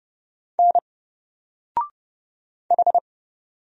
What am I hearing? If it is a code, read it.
Morse code, heard as a keyed tone, about 29 wpm, 700 Hz. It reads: N H
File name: Single-Letter-Number-30wpm